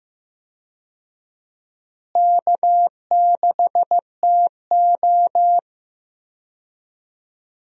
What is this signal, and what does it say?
Morse code, heard as a keyed tone, about 15 wpm, 700 Hz. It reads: K6TO
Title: USA Call Signs - 4 Characters - Repeated 2x - 15wpm